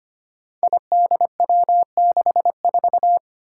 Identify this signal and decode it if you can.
Morse code, heard as a keyed tone, about 25 wpm, 700 Hz. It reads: IDW64
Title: Five Challenging Letter-Number Combinations 25wpm